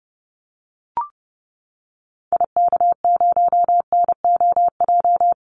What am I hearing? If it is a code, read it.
Morse code, heard as a keyed tone, about 30 wpm, 700 Hz. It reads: IK0NOJ
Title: Call Signs Repeated 2x - 30wpm